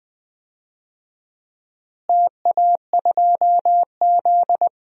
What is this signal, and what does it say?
Morse code, heard as a keyed tone, about 20 wpm, 700 Hz. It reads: TA2Z
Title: DXCC-100 Call Signs - 4 Characters - Repeated 3x - 20wpm